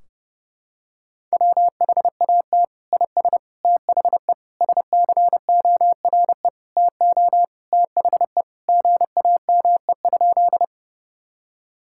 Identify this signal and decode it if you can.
Morse code, heard as a keyed tone, about 30 wpm, 700 Hz. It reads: WHAT IS THE SCORE TO THE GAME?
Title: Sentences from Top 1600 Words 30wpm